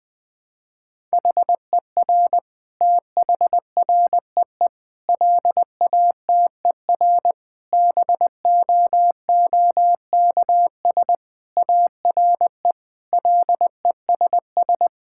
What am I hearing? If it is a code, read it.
Morse code, heard as a keyed tone, about 20 wpm, 700 Hz. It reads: HER THREE LATER BOOKS ARE LESS
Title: Sentences from Top 1000 Words - Review Part 6 - 20wpm